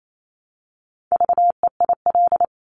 Morse code, heard as a keyed tone, about 28 wpm, 700 Hz. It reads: VEIL
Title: Multi-letter Phonemes Repeated 3x 28wpm